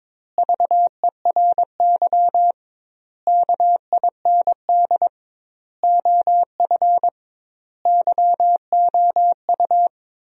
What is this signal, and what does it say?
Morse code, heard as a keyed tone, about 22 wpm, 700 Hz. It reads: VERY KIND OF YOU
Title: 2x Word-Spacing - Sentences from Top 300 Words - 22wpm